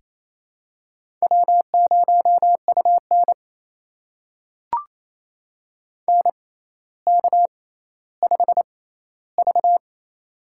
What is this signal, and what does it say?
Morse code, heard as a keyed tone, about 28 wpm, spaced as for 14 wpm, 700 Hz. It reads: W0UN NK5V
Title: USA Call Signs - 4 Characters - 6x Character Spacing - 28wpm